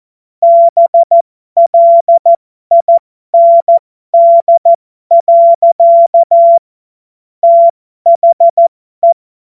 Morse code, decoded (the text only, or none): BLIND. THE